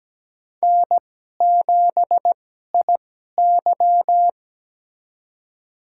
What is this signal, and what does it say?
Morse code, heard as a keyed tone, about 17 wpm, spaced as for 12 wpm, 700 Hz. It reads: N7IY
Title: USA Call Signs - 4 Characters - 2x Character Spacing - 17wpm